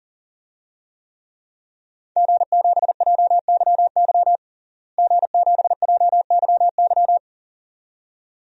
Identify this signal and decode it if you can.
Morse code, heard as a keyed tone, about 40 wpm, 700 Hz. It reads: G7JYY G7JYY
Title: DXCC-100 Call Signs - 5 Characters - Repeated 2x - 40wpm